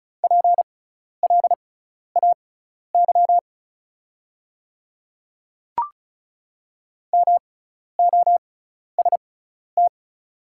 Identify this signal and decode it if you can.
Morse code, heard as a keyed tone, about 36 wpm, spaced as for 12 wpm, 690 Hz. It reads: PLAY MOST
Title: 6x Character Spacing - Top 200 Words - 35wpm